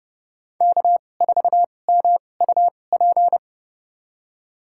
Morse code, heard as a keyed tone, about 30 wpm, spaced as for 22 wpm, 700 Hz. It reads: K4MUP
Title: USA Call Signs - 2x Character Spacing - 30wpm